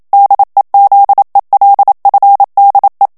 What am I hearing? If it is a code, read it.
Morse code, heard as a keyed tone, about 27 wpm, 800 Hz. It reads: DEZELFDE
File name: HOLLANDAIS texte clair 1 vitesse 26 mots minute